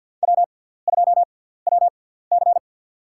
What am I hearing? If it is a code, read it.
Morse code, heard as a keyed tone, about 47 wpm, spaced as for 17 wpm, 700 Hz. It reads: W2WC
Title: USA Call Signs - 6x Character Spacing - 50wpm